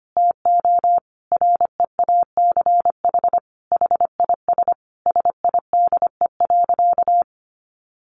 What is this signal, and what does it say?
Morse code, heard as a keyed tone, about 25 wpm, 700 Hz. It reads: TO REACH HIS SIDE.
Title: Sentences from Top 400 Words 25wpm